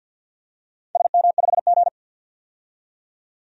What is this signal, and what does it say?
Morse code, heard as a keyed tone, about 47 wpm, 700 Hz. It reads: SM5G
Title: DXCC-100 Call Signs - 4 Characters - 50wpm